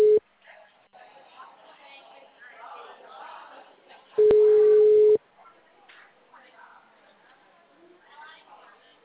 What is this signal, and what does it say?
Ringback tone (Europe, 425 Hz), with speech.